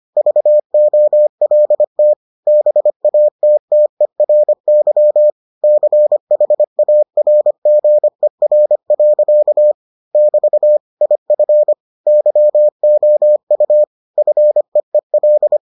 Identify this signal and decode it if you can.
Morse code, heard as a keyed tone, about 25 wpm, 600 Hz. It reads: VOLT BATTERY CHARGER. <BT> IF YOU FEEL